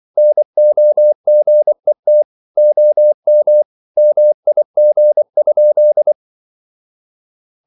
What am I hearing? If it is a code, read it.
Morse code, heard as a keyed tone, about 24 wpm, 600 Hz. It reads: NOGET OM MIG?
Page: Ferie 24 wpm